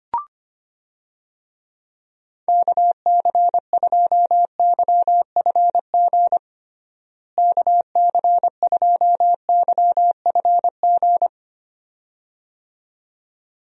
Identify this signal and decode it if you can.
Morse code, heard as a keyed tone, about 25 wpm, 700 Hz. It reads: KC2YFG KC2YFG
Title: USA Call Signs - 6 Characters - Repeated 2x - 25wpm